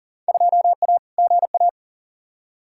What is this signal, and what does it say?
Morse code, heard as a keyed tone, about 40 wpm, 700 Hz. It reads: 2A GA
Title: Field Day Exchanges Encore - 40wpm